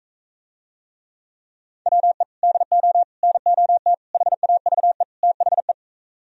Morse code, heard as a keyed tone, about 45 wpm, 700 Hz. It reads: WE DO NOT HAVE THE